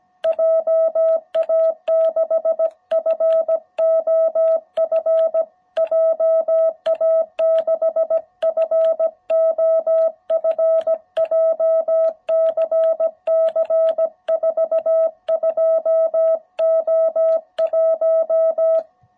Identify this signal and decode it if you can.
Morse code, heard as a keyed tone, about 17 wpm, 660 Hz. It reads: JA6FOF JA6FOF JCC 42O1